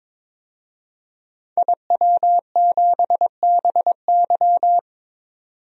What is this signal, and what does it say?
Morse code, heard as a keyed tone, about 22 wpm, 700 Hz. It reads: IW7BY
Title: DXCC-100 Call Signs - 5 Characters - 22wpm